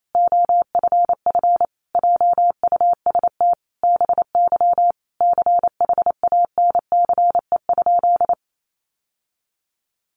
Morse code, heard as a keyed tone, about 28 wpm, 700 Hz. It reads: OFF JUST BY CHANCE?
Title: Sentences from Top 800 Words 28wpm